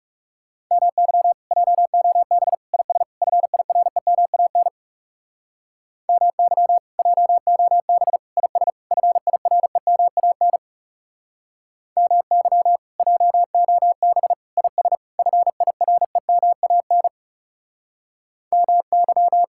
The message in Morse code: MY JOB IS FIREMAN MY JOB IS FIREMAN MY JOB IS FIREMAN MY

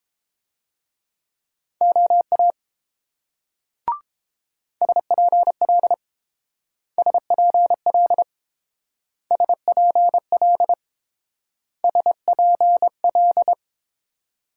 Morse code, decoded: OA SPL SPL SPL SPL

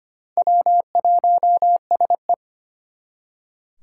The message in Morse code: W1SE

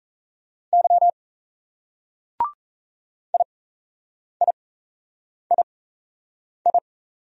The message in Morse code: Y IIII

